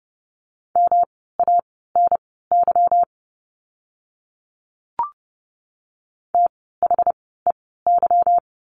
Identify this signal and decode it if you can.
Morse code, heard as a keyed tone, about 30 wpm, spaced as for 17 wpm, 700 Hz. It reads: MANY THEY
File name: course / 30:10wpm / 237 Lesson 38f - Letter q - Words 30wpm